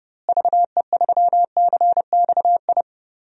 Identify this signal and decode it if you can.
Morse code, heard as a keyed tone, about 32 wpm, 700 Hz. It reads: VE3CXI
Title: DXCC-100 Call Signs - 6 Characters - Repeated 3x - 30wpm